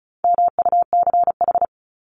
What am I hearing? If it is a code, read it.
Morse code, heard as a keyed tone, about 34 wpm, 700 Hz. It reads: MUCH
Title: Top 200 Words 35wpm